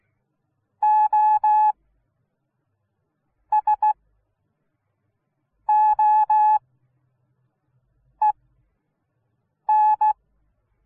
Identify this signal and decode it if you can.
Morse code, heard as a keyed tone, about 16 wpm, spaced as for 5 wpm, 850 Hz. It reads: OSOEN